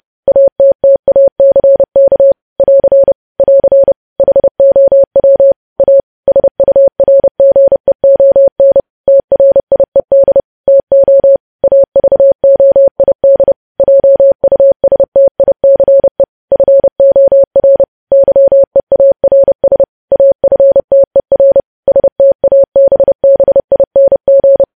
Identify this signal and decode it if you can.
Morse code, heard as a keyed tone, about 30 wpm, 550 Hz. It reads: ATTACK <AR> <AR> HOW A SURGEON TRIED TO AVOID JUSTICE FOR YEARS AFTER STABBING
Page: news headlines in morse code at 30 wpm